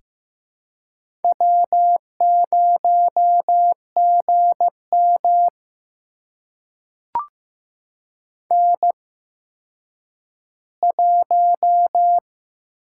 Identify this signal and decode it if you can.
Morse code, heard as a keyed tone, about 15 wpm, spaced as for 9 wpm, 700 Hz. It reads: W0GM N1
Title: USA Call Signs - 4 Characters - 8x Character Spacing - 15wpm